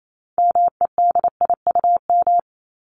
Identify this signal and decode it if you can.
Morse code, heard as a keyed tone, about 28 wpm, 700 Hz. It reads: MEDIUM
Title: Top 751-1000 Words 28wpm